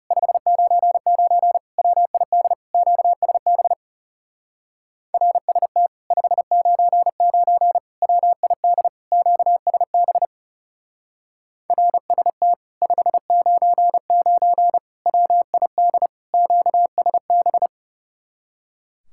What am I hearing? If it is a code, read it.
Morse code, heard as a keyed tone, about 39 wpm, 700 Hz. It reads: 599 WID QSB RST 599 WID QSB RST 599 WID QSB